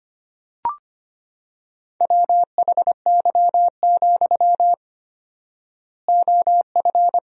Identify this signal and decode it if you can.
Morse code, heard as a keyed tone, about 25 wpm, 700 Hz. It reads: WHY, OF
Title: 4x Word-Spacing - Sentences from Top 500 Words - 25wpm